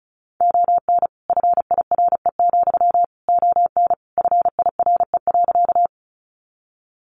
Morse code, heard as a keyed tone, about 34 wpm, 700 Hz. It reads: ON FIRE, ON FIRE.
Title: Sentences from Top 400 Words 35wpm